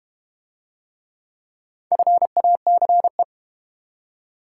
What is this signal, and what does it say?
Morse code, heard as a keyed tone, about 31 wpm, 700 Hz. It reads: FACE
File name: Easy Words - Speed-Racing with Hints - 25wpm